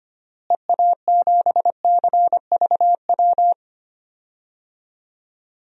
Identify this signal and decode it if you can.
Morse code, heard as a keyed tone, about 25 wpm, 700 Hz. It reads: EA7CVW
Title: DXCC-100 Call Signs - 6 Characters - Repeated 3x - 25wpm